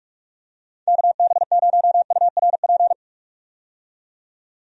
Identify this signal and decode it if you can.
Morse code, heard as a keyed tone, about 45 wpm, 700 Hz. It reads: KB0URP
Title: USA Call Signs - 6 Characters - 45wpm